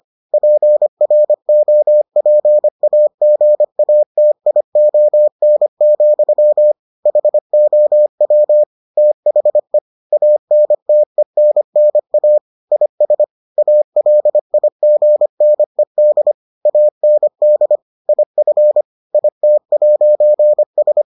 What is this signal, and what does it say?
Morse code, heard as a keyed tone, about 25 wpm, 600 Hz. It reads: PROPAGATION, HOW THE ANTENNA IS ALIGNED AND IF IT'S